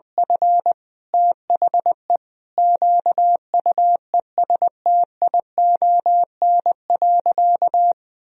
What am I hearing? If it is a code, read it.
Morse code, heard as a keyed tone, about 20 wpm, 700 Hz. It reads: F THE QUESTION.
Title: Sentences from Top 1000 Words - Review Part 4 - 20wpm